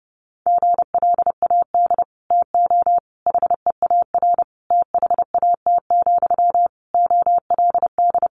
Morse code, heard as a keyed tone, about 30 wpm, 700 Hz. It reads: GLAD TO HEAR THAT, OLD
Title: Sentences from Top 800 Words 30wpm